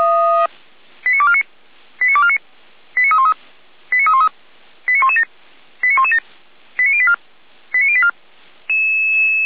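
Five-tone selective call (ZVEI), heard as quick runs of five tones, tones of about 70 ms. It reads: 89249 89249 89313 89313 89117 89117 89084 89084
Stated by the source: ZVEI five-tone selective call received on an SDR.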